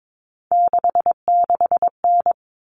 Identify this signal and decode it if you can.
Morse code, heard as a keyed tone, about 22 wpm, 700 Hz. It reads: 66N
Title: All Cut Numbers - 3-Digit 22wpm